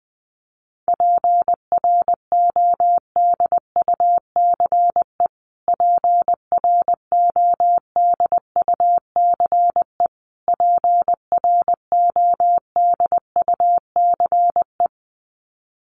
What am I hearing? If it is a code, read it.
Morse code, heard as a keyed tone, about 20 wpm, 700 Hz. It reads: PRODUCE PRODUCE PRODUCE
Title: Top 500 Words Repeated 20wpm